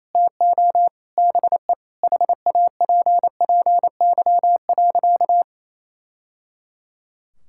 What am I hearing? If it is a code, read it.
Morse code, heard as a keyed tone, about 28 wpm, 700 Hz. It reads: TO BE HAPPY.
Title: Sentences from Top 1000 Words - Review Part 5 - 28wpm